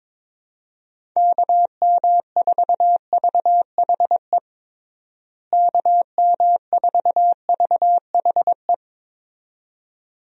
Morse code, heard as a keyed tone, about 22 wpm, 700 Hz. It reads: KM4VHE KM4VHE
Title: USA Call Signs - 6 Characters - Repeated 2x - 22wpm